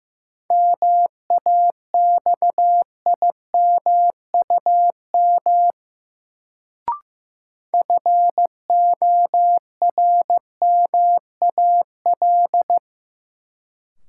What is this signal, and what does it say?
Morse code, heard as a keyed tone, about 15 wpm, 700 Hz. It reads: MAXIMUM FORMAL
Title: Sentences from Top 1300 Words - New Word Drill Alt - 15wpm